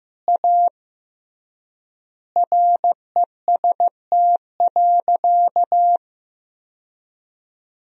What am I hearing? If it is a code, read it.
Morse code, heard as a keyed tone, about 15 wpm, 700 Hz. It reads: A REST.
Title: 3x Word-Spacing - Sentences from Top 500 Words - 15wpm